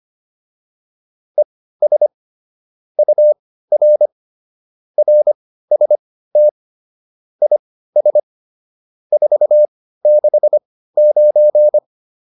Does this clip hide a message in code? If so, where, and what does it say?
Yes: Morse code from 1.4 to 11.8 s, heard as a keyed tone, about 25 wpm, spaced as for 15 wpm, 600 Hz. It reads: ES UR RST IS 469